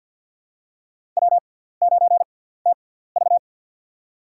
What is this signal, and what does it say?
Morse code, heard as a keyed tone, about 48 wpm, spaced as for 17 wpm, 700 Hz. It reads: W9TV
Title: USA Call Signs - 4 Characters - 6x Character Spacing - 50wpm